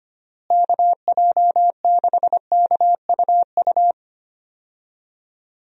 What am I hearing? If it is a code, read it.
Morse code, heard as a keyed tone, about 25 wpm, 700 Hz. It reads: KJ6KUU
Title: USA Call Signs - 6 Characters - Repeated 3x - 25wpm